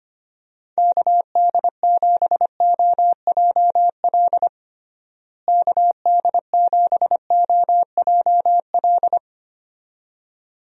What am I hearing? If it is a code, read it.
Morse code, heard as a keyed tone, about 25 wpm, 700 Hz. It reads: KD7OJL KD7OJL
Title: USA Call Signs - 6 Characters - Repeated 2x - 25wpm